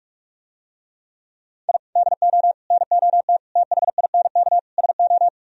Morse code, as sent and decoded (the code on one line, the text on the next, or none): .. / -.. --- / -. --- - / - .... .. -. -.- / ... ---
I DO NOT THINK SO